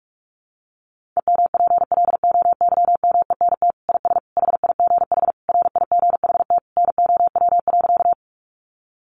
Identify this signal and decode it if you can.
Morse code, heard as a keyed tone, about 44 wpm, 700 Hz. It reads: EMPLOYMENT IS HIGH RIGHT NOW.
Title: Sentences from Top 1500 Words 45wpm